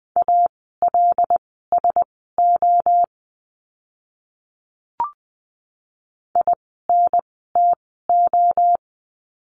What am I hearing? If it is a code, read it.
Morse code, heard as a keyed tone, about 20 wpm, spaced as for 14 wpm, 700 Hz. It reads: ALSO INTO
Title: 160 Lesson 26f - Letter g - Words 20wpm